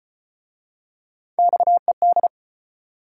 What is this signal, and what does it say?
Morse code, heard as a keyed tone, about 35 wpm, 700 Hz. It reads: XED